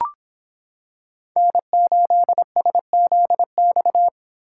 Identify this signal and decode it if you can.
Morse code, heard as a keyed tone, about 24 wpm, 700 Hz. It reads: N8SZX